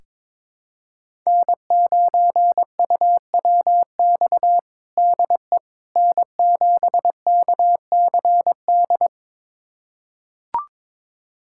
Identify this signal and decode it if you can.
Morse code, heard as a keyed tone, about 22 wpm, 700 Hz. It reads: N9UWX DE N7KCD
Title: USA Callsign DE Callsign - 5 Characters - 22wpm